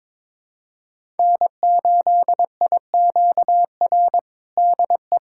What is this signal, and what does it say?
Morse code, heard as a keyed tone, about 23 wpm, 700 Hz. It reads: N8IQR DE
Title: USA Callsign DE Callsign - 5 Characters - 22wpm